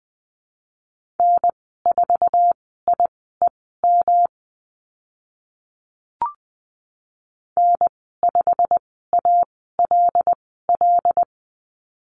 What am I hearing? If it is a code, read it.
Morse code, heard as a keyed tone, about 20 wpm, spaced as for 15 wpm, 700 Hz. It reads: N4IEM N5ALL
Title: 122 Lesson 20g - Number 3 - Call Signs 20wpm